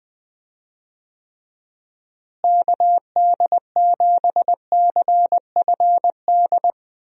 Morse code, heard as a keyed tone, about 20 wpm, 700 Hz. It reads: KD7CFD
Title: USA Call Signs - 6 Characters - Repeated 3x - 20wpm